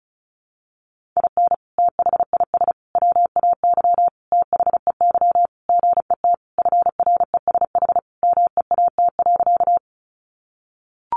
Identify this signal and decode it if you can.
Morse code, heard as a keyed tone, about 34 wpm, 700 Hz. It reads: IN THIS WAY THEY GET FRESH MEAT.
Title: Sentences from Top 1200 Words 35wpm